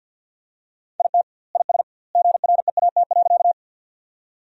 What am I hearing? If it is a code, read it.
Morse code, heard as a keyed tone, about 46 wpm, 700 Hz. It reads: IT IS GREAT.
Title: 2x Word-Spacing - Sentences from Top 200 Words - 50wpm